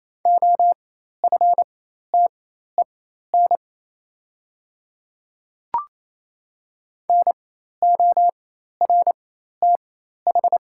Morse code, heard as a keyed tone, about 28 wpm, spaced as for 13 wpm, 690 Hz. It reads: OFTEN NORTH
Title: 4x Character Spacing - Top 300 Words - 28wpm